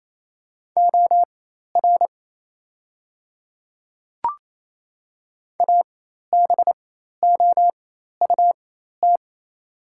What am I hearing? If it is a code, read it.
Morse code, heard as a keyed tone, about 27 wpm, spaced as for 13 wpm, 700 Hz. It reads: OR ABOUT
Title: 4x Character Spacing - Top 100 Words - 28wpm